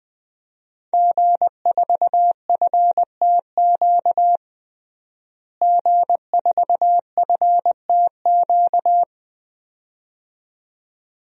Morse code, heard as a keyed tone, about 20 wpm, 700 Hz. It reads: G4FTQ G4FTQ
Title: DXCC-100 Call Signs - 5 Characters - Repeated 2x - 20wpm